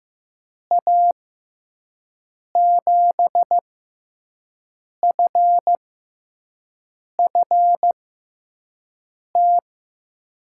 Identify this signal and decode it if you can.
Morse code, heard as a keyed tone, about 15 wpm, spaced as for 5 wpm, 700 Hz. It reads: A7FFT
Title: USA Call Signs - 6 Characters - 6x Character Spacing - 15wpm